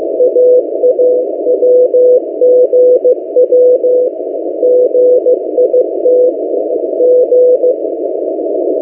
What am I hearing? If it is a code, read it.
Morse code, heard as a keyed tone, about 15 wpm, 500 Hz. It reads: AAWGW GITG
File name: M08a- -1355utc-8096khz